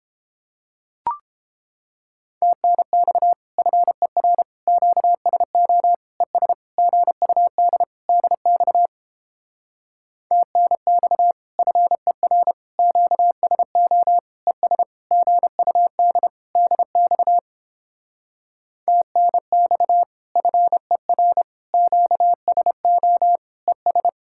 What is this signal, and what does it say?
Morse code, heard as a keyed tone, about 29 wpm, 700 Hz. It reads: TNX FER QSO ES GUD DX TNX FER QSO ES GUD DX TNX FER QSO ES